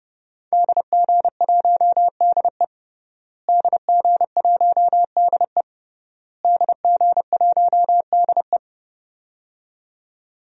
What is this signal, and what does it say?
Morse code, heard as a keyed tone, about 30 wpm, 700 Hz. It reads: DG1DE DG1DE DG1DE
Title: DXCC-100 Call Signs - 5 Characters - Repeated 3x - 30wpm